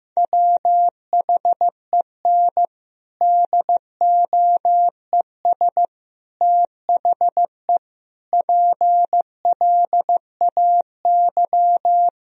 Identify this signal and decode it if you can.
Morse code, heard as a keyed tone, about 15 wpm, 700 Hz. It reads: WHEN DOES THE PLAY
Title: Sentences from Top 1000 Words - Review Part 1 - 15wpm